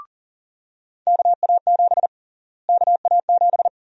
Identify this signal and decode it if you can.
Morse code, heard as a keyed tone, about 40 wpm, 700 Hz. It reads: KA7 KA7
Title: USA Call Sign Prefixes - Repeated 2x - 40wpm